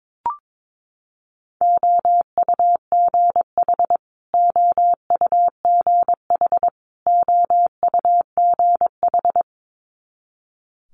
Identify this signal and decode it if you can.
Morse code, heard as a keyed tone, about 22 wpm, 700 Hz. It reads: OUGH OUGH OUGH
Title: Multi-letter Phonemes Repeated 3x 22wpm